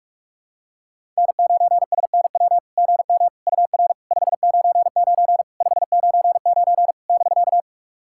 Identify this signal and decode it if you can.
Morse code, heard as a keyed tone, about 44 wpm, 700 Hz. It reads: N9SNW GM UR 599 599 <BK>
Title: SOTA QSO Practice 45wpm